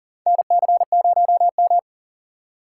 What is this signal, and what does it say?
Morse code, heard as a keyed tone, about 39 wpm, 700 Hz. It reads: NC0M
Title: USA Call Signs - 4 Characters - 3x Character Spacing - 40wpm